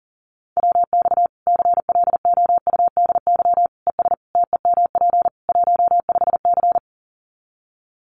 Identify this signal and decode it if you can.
Morse code, heard as a keyed tone, about 40 wpm, 700 Hz. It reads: WX CLOUDY ES TEMP 15C
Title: Common QSO Phrases 40wpm